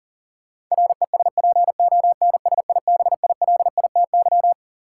Morse code, heard as a keyed tone, about 40 wpm, 700 Hz. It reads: RESPONSIBILITY
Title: Sentences from Top 1200 Words - New Word Drill Alt - 40wpm